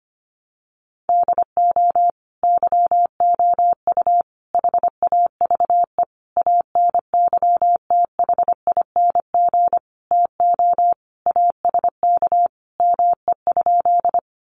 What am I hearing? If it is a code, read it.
Morse code, heard as a keyed tone, about 25 wpm, 700 Hz. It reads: DO YOU HAVE ANYTHING TO ASK ME?
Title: Sentences from Top 1000 Words 25wpm